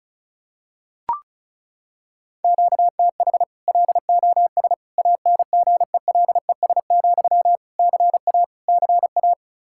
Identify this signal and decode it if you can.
Morse code, heard as a keyed tone, about 36 wpm, 700 Hz. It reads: QTH LOS ANGELES, CA CA
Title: Practice QSO - Exchange 1 - Beginning - 35wpm